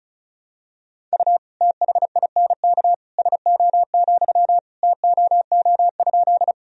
Morse code, heard as a keyed tone, about 34 wpm, 690 Hz. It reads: U THINK SO, TOO?
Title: Sentences from Top 200 Words and Abbreviations 35wpm